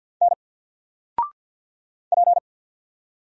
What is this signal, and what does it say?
Morse code, heard as a keyed tone, about 48 wpm, 700 Hz. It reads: N P
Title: Single Letter-Number 50wpm